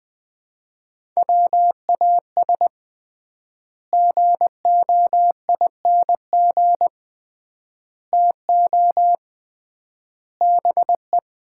Morse code, heard as a keyed tone, about 20 wpm, 700 Hz. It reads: WAS GOING TO BE
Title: 3x Word-Spacing - Sets of 4 Words - 20wpm